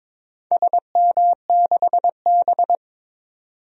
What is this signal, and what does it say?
Morse code, heard as a keyed tone, about 22 wpm, 700 Hz. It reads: SM6B
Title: DXCC-100 Call Signs - 4 Characters - 22wpm